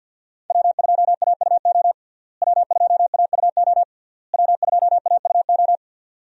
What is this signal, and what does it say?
Morse code, heard as a keyed tone, about 49 wpm, 700 Hz. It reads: W2AUO W2AUO W2AUO
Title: USA Call Signs - 5 Characters - Repeated 3x - 50wpm